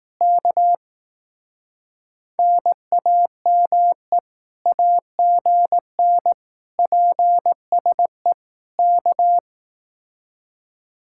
Morse code, heard as a keyed tone, about 18 wpm, 700 Hz. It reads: K NAME AGN PSE K
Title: Speed-Racing Common QSO Phrases Encore - 15wpm